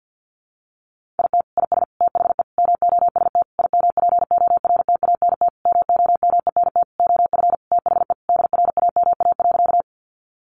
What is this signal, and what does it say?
Morse code, heard as a keyed tone, about 50 wpm, 700 Hz. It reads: IT IS THE MOST IMPORTANT MOMENT OF THE DRAMA.
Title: Sentences from Top 1500 Words 50wpm